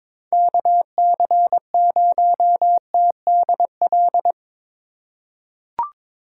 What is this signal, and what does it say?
Morse code, heard as a keyed tone, about 22 wpm, 700 Hz. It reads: KC0TDL
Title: USA Call Signs - 3x Character Spacing - 22wpm